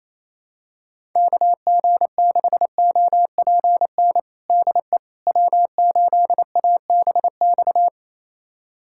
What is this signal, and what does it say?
Morse code, heard as a keyed tone, about 28 wpm, 700 Hz. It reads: KG6OPN DE W8ABX
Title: USA Callsign DE Callsign 28wpm